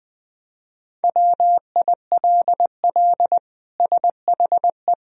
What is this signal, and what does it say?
Morse code, heard as a keyed tone, about 21 wpm, 700 Hz. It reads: WILL SHE
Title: Sentences from Top 1000 Words - Review Part 2 - 20wpm